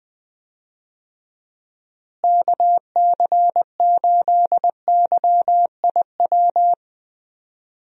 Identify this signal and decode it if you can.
Morse code, heard as a keyed tone, about 21 wpm, 700 Hz. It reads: KC8YIW
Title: USA Call Signs - 6 Characters - Repeated 3x - 20wpm